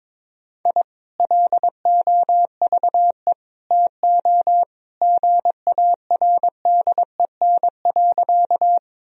Morse code, heard as a keyed tone, about 22 wpm, 700 Hz. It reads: I LOVE TO GARDEN.
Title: Sentences from Top 1000 Words - Review Part 4 - 22wpm